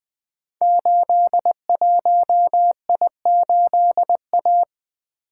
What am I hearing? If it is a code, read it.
Morse code, heard as a keyed tone, about 20 wpm, 700 Hz. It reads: 81I8A
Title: Code Groups - 5 Letter:Numbers - 20wpm